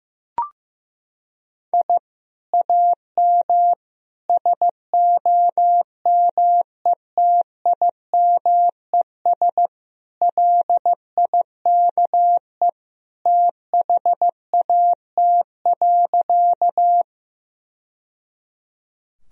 Morse code, heard as a keyed tone, about 15 wpm, 700 Hz. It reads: I AM SOMETIMES LIKE THAT.
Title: Sentences from Top 1000 Words - Review Part 3 - 15wpm